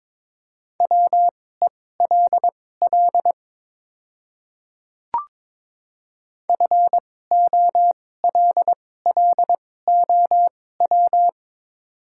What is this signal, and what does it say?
Morse code, heard as a keyed tone, about 22 wpm, spaced as for 16 wpm, 700 Hz. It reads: WELL FOLLOW
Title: 2x Character Spacing - Top 300 Words - 22wpm